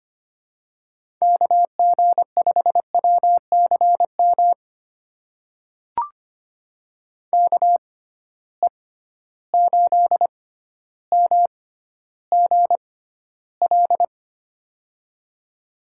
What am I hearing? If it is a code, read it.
Morse code, heard as a keyed tone, about 25 wpm, spaced as for 13 wpm, 700 Hz. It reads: KG5WCM KE8MGL
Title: USA Call Signs - 6 Characters - 6x Character Spacing - 25wpm